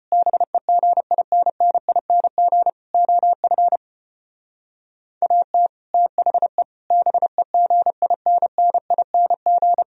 Morse code, heard as a keyed tone, about 33 wpm, 700 Hz. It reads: BEGINNING OF AT THE BEGINNING